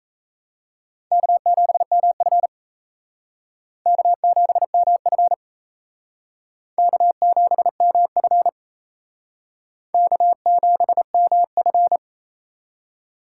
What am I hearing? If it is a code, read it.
Morse code, heard as a keyed tone, about 35 wpm, 700 Hz. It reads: K7MF K7MF K7MF K7MF
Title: Speed Racing - USA 1x2 Call Signs 28wpm